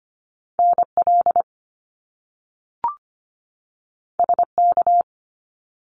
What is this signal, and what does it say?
Morse code, heard as a keyed tone, about 25 wpm, 700 Hz. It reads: NL SK
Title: Canadian Provinces and Territories 25wpm